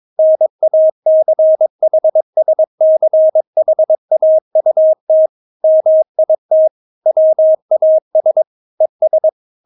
Morse code, heard as a keyed tone, about 22 wpm, 630 Hz. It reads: NACHSCHAUT MIT WAS ES
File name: Witz3 22wpm